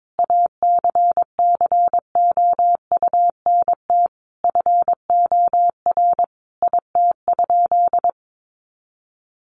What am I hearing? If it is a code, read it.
Morse code, heard as a keyed tone, about 22 wpm, 700 Hz. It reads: ACCOUNT FOR IT?
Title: Sentences from Top 900 Words 22wpm